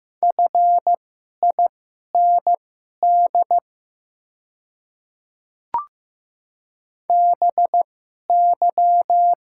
Morse code, heard as a keyed tone, about 15 wpm, spaced as for 11 wpm, 700 Hz. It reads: FIND BY
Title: 2x Character Spacing - Top 100 Words - 15wpm